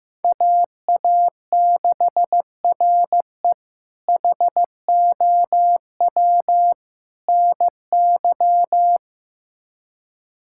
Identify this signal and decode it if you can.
Morse code, heard as a keyed tone, about 15 wpm, 700 Hz. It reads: AA6RE HOW NY
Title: K1USN SST QSO Practice - 15wpm